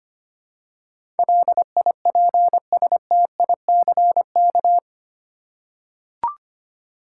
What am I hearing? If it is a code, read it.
Morse code, heard as a keyed tone, about 25 wpm, 700 Hz. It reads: LIPSTICK
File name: practice / 25wpm / Easy Words with Hints - 25wpm